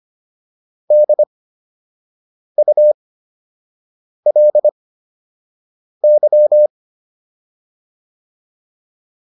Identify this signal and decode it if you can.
Morse code, heard as a keyed tone, about 25 wpm, spaced as for 6 wpm, 600 Hz. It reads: DULY